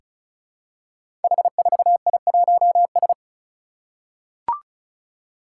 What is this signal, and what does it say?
Morse code, heard as a keyed tone, about 34 wpm, 700 Hz. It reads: H4I1S
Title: Five Challenging Letter-Number Combinations 35wpm